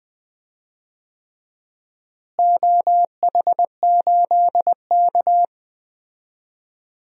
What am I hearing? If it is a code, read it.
Morse code, heard as a keyed tone, about 20 wpm, 700 Hz. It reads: OH8K
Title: DXCC-100 Call Signs - 4 Characters - Repeated 3x - 20wpm